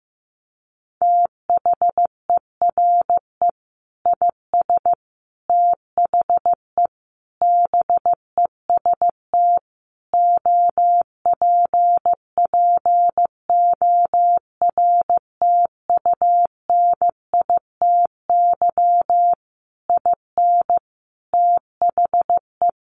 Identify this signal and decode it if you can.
Morse code, heard as a keyed tone, about 14 wpm, 700 Hz. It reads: THERE IS THE BEST OPPORTUNITY IN THE
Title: Sentences from Top 1500 Words 15wpm